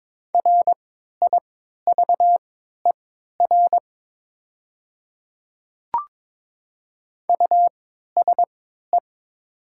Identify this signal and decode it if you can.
Morse code, heard as a keyed tone, about 22 wpm, spaced as for 12 wpm, 700 Hz. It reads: RIVER USE
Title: 3x Character Spacing - Top 300 Words - 22wpm